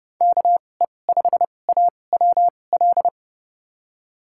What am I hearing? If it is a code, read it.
Morse code, heard as a keyed tone, about 30 wpm, spaced as for 22 wpm, 700 Hz. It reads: KE5AWL
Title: USA Call Signs - 6 Characters - 2x Character Spacing - 30wpm